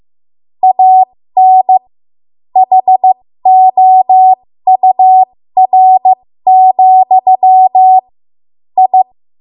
Morse code, heard as a keyed tone, about 15 wpm, spaced as for 13 wpm, 750 Hz. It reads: AN HOUR, I
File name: Morse / 13WPM